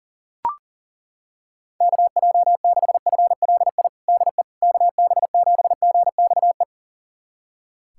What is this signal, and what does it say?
Morse code, heard as a keyed tone, about 40 wpm, 700 Hz. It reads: KJ6FLI DE KB7GXE
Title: USA Callsign DE Callsign - 6 Characters - 40wpm